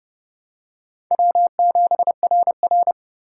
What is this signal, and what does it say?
Morse code, heard as a keyed tone, about 31 wpm, 700 Hz. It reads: W7RR
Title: USA Call Signs - 4 Characters - 3x Character Spacing - 30wpm